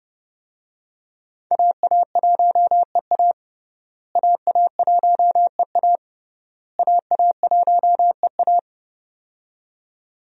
Morse code, heard as a keyed tone, about 32 wpm, 700 Hz. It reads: AA1EA AA1EA AA1EA
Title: USA Call Signs - 5 Characters - Repeated 3x - 30wpm